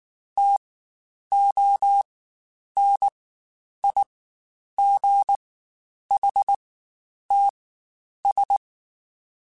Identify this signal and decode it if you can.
Morse code, heard as a keyed tone, about 19 wpm, spaced as for 9 wpm, 800 Hz. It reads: TONIGHTS
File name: Spaced Characters / Spaced Characters-mp3s / Weather5 22 sp @18